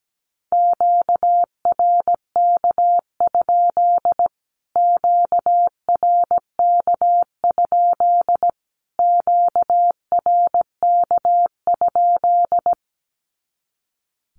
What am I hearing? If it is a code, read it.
Morse code, heard as a keyed tone, about 17 wpm, 700 Hz. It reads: QRK? QRK? QRK?
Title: Q-Codes Repeated 3x 17wpm